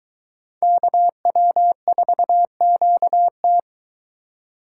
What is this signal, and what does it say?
Morse code, heard as a keyed tone, about 22 wpm, 700 Hz. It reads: KW4QT